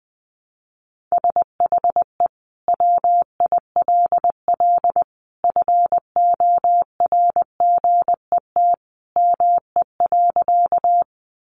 Morse code, heard as a keyed tone, about 20 wpm, 700 Hz. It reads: SHE WILL FORGET ME.
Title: Sentences from Top 1200 Words 20wpm